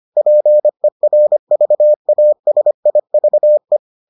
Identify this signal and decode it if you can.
Morse code, heard as a keyed tone, about 25 wpm, 600 Hz. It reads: PERVASIVE